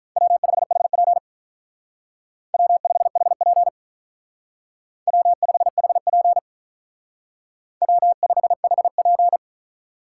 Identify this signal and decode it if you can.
Morse code, heard as a keyed tone, about 45 wpm, 700 Hz. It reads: W5HP W5HP W5HP W5HP